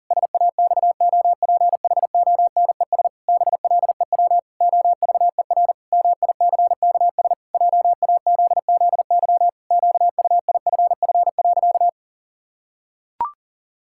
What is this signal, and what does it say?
Morse code, heard as a keyed tone, about 40 wpm, 700 Hz. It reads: SAXOPHONES BLEW OVER MICKS JAZZY QUIFF.
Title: Pangram Challenge v2 - 40wpm